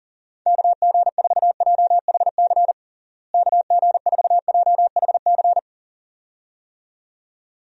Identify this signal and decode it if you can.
Morse code, heard as a keyed tone, about 40 wpm, 700 Hz. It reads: KG4JHC KG4JHC
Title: USA Call Signs - 6 Characters - Repeated 2x - 40wpm